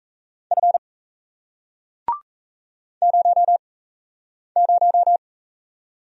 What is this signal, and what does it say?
Morse code, heard as a keyed tone, about 38 wpm, 700 Hz. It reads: F 0 0